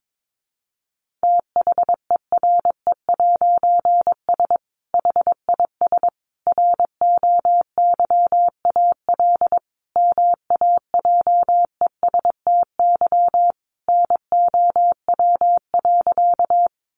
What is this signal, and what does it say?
Morse code, heard as a keyed tone, about 22 wpm, 700 Hz. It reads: THERE'S HIS ROYAL MAJESTY NOW.
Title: A Sound Of Thunder 22wpm